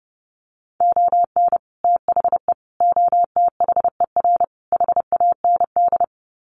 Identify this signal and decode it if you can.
Morse code, heard as a keyed tone, about 30 wpm, 700 Hz. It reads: ON THE OTHER HAND
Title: Sets of 4 Words 30wpm